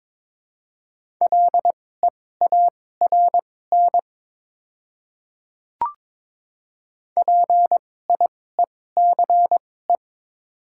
Morse code, heard as a keyed tone, about 22 wpm, spaced as for 16 wpm, 700 Hz. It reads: LEARN PIECE
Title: 2x Character Spacing - Top 500 Words - 22wpm